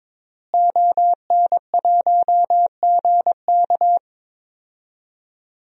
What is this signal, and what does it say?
Morse code, heard as a keyed tone, about 21 wpm, 700 Hz. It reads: ON1GK
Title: DXCC-100 Call Signs - 5 Characters - Repeated 2x - 22wpm